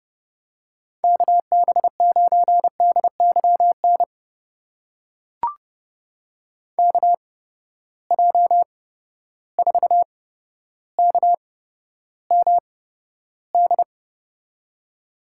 Morse code, heard as a keyed tone, about 31 wpm, spaced as for 13 wpm, 700 Hz. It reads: KB9DYN KJ4KMD
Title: USA Call Signs - 6 Characters - 8x Character Spacing - 30wpm